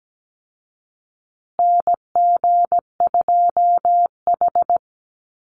Morse code, heard as a keyed tone, about 17 wpm, 700 Hz. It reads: NG2H
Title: Call Signs Repeated 2x - 17wpm